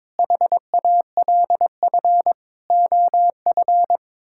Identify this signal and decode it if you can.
Morse code, heard as a keyed tone, about 22 wpm, 700 Hz. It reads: HALF OF
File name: practice / 22wpm / Build A Phrase - 5-Word Phrases - 22wpm